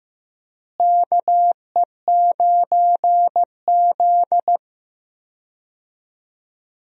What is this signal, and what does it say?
Morse code, heard as a keyed tone, about 15 wpm, 700 Hz. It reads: KE9Z
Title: USA Call Signs - 4 Characters - Repeated 2x - 15wpm